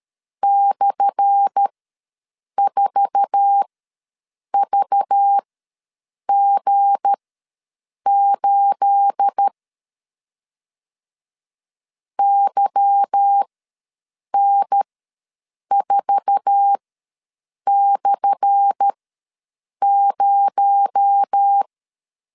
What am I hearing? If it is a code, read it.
Morse code, heard as a keyed tone, about 13 wpm, spaced as for 7 wpm, 800 Hz. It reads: /4VG8 YN4/0
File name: LEZIONI / 10 SIGNS / 10A SLASH ws 12wpm 3443